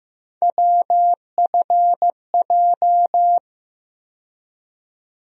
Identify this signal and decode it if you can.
Morse code, heard as a keyed tone, about 15 wpm, 700 Hz. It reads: WFJ
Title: USA Call Sign Postfixes - 3 Letters 15wpm